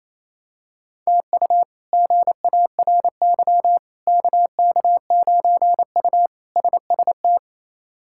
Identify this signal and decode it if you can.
Morse code, heard as a keyed tone, about 28 wpm, 700 Hz. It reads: TU GARY KK9U SST
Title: K1USN SST QSO Practice - 28wpm